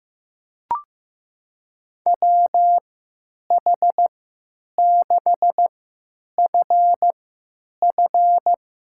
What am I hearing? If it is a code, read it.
Morse code, heard as a keyed tone, about 15 wpm, spaced as for 9 wpm, 700 Hz. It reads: WH6FF
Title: USA Call Signs - 3x Character Spacing - 15wpm